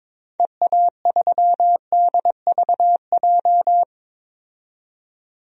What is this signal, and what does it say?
Morse code, heard as a keyed tone, about 21 wpm, 700 Hz. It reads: EA3DVJ
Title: DXCC-100 Call Signs - 6 Characters - 22wpm